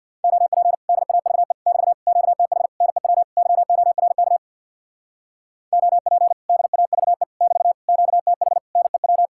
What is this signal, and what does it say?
Morse code, heard as a keyed tone, about 67 wpm, 700 Hz. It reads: OP DAVE <BT> QTH NEW YORK OP DAVE <BT> QTH NEW